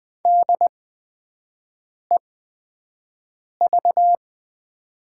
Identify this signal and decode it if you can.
Morse code, heard as a keyed tone, about 20 wpm, spaced as for 5 wpm, 700 Hz. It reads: DEV
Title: 8x Character Spacing - Top 500 Words - 20wpm